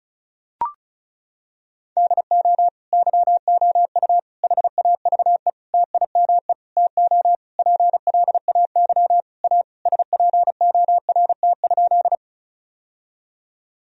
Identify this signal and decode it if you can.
Morse code, heard as a keyed tone, about 36 wpm, 700 Hz. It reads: DO YOU HAVE TIME TO PLAY A SPORT?
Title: Sentences from Top 1000 Words - Review Part 4 - 35wpm